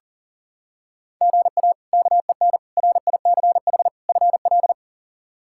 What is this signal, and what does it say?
Morse code, heard as a keyed tone, about 40 wpm, 700 Hz. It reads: GA KEN RICH FL
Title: K1USN SST QSO Practice - 40wpm